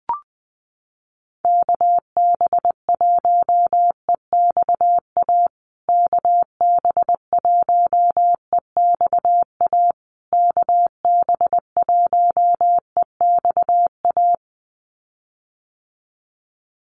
Morse code, heard as a keyed tone, about 20 wpm, 700 Hz. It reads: KB1EXA KB1EXA KB1EXA
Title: USA Call Signs Repeated 3x 20wpm